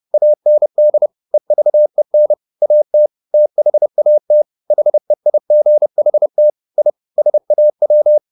Morse code, heard as a keyed tone, about 30 wpm, 600 Hz. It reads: AND EVEN AT THAT HEIGHT I SAW
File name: the old man and the sea 2 30wpm